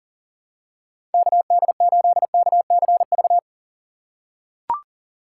Morse code, heard as a keyed tone, about 40 wpm, 700 Hz. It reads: KD8KCV
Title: USA Call Signs - 6 Characters - 2x Character Spacing - 40wpm